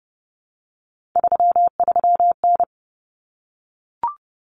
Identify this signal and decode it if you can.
Morse code, heard as a keyed tone, about 30 wpm, 700 Hz. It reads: 33N
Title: Common Cut Numbers - 3-Digit 30wpm